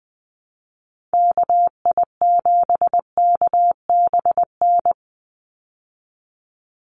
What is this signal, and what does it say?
Morse code, heard as a keyed tone, about 20 wpm, 700 Hz. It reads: KI7KBN
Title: USA Call Signs 20wpm